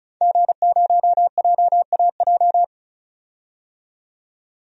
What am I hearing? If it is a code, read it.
Morse code, heard as a keyed tone, about 34 wpm, 700 Hz. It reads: G0JAJ
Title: DXCC-100 Call Signs - 5 Characters - 35wpm